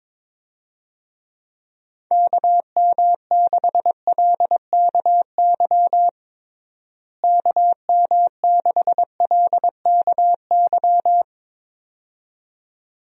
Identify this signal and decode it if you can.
Morse code, heard as a keyed tone, about 22 wpm, 700 Hz. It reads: KM6LKY KM6LKY
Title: USA Call Signs - 6 Characters - Repeated 2x - 22wpm